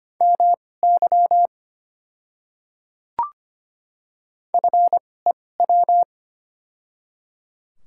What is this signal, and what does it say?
Morse code, heard as a keyed tone, about 25 wpm, spaced as for 18 wpm, 700 Hz. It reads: MY FEW
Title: 2x Character Spacing - Top 300 Words - 25wpm